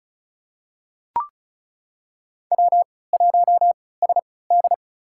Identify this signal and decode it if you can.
Morse code, heard as a keyed tone, about 36 wpm, spaced as for 20 wpm, 700 Hz. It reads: W1SD
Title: USA Call Signs - 4 Characters - 3x Character Spacing - 35wpm